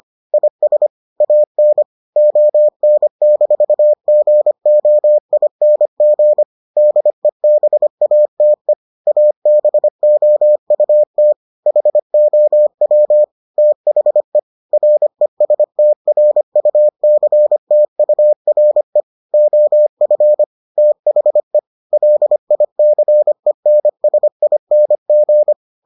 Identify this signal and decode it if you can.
Morse code, heard as a keyed tone, about 25 wpm, 600 Hz. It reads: IS AN ON-GOING DEBATE ABOUT HOW THE RESTRUCTURE OF THE LICENSING